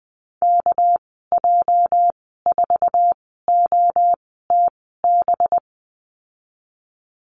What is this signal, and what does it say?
Morse code, heard as a keyed tone, about 20 wpm, spaced as for 14 wpm, 700 Hz. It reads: KJ4OTB
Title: 213 Lesson 34g - Number 8 - Call Signs 20wpm